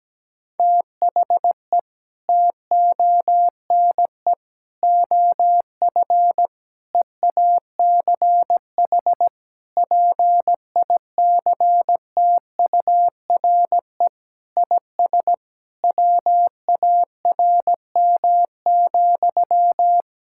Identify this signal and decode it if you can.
Morse code, heard as a keyed tone, about 17 wpm, 700 Hz. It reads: THE TONE OF EACH PICTURE IS WARM,
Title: Sentences from Top 1600 Words 17wpm